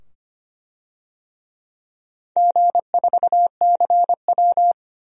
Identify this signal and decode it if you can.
Morse code, heard as a keyed tone, about 25 wpm, 700 Hz. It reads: G4CW
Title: DXCC-100 Call Signs - 4 Characters - 25wpm